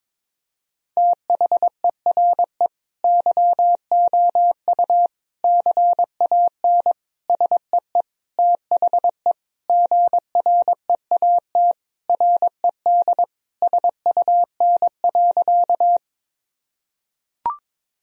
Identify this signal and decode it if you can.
Morse code, heard as a keyed tone, about 21 wpm, 700 Hz. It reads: THERE YOU CAN SEE THE GREAT RED SUN.
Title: Sentences from Top 1000 Words - Review Part 2 - 22wpm